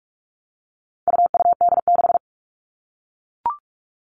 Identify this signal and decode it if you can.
Morse code, heard as a keyed tone, about 44 wpm, 700 Hz. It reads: UUD6
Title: All Cut Numbers - 4-Digit 45wpm